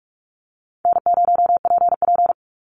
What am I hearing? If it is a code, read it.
Morse code, heard as a keyed tone, about 44 wpm, 700 Hz. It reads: N0PP
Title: Easy Call Signs Repeated 2x - 45wpm